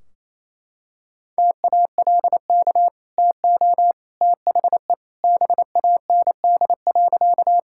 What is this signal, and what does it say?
Morse code, heard as a keyed tone, about 28 wpm, 700 Hz. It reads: TALK TO THE BAND.
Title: Sentences from Top 1600 Words 28wpm